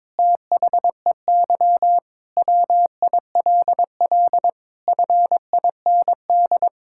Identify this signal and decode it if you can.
Morse code, heard as a keyed tone, about 22 wpm, 700 Hz. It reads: THEY WILL FIND
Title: Sentences from Top 1000 Words - Review Part 6 - 22wpm